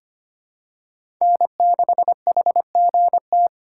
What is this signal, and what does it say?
Morse code, heard as a keyed tone, about 25 wpm, 700 Hz. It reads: N6HGT
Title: USA Call Signs - 5 Characters - 2x Character Spacing - 25wpm